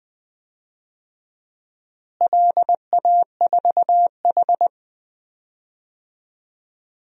Morse code, heard as a keyed tone, about 20 wpm, 700 Hz. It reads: LA4H
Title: DXCC-100 Call Signs - 4 Characters - Repeated 2x - 20wpm